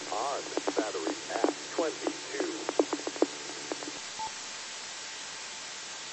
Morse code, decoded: none

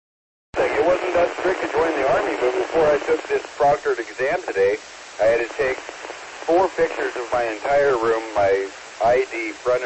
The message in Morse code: none

